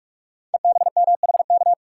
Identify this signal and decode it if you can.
Morse code, heard as a keyed tone, about 44 wpm, 700 Hz. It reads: EBMHK
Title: Code Groups - 5 Letters - 45wpm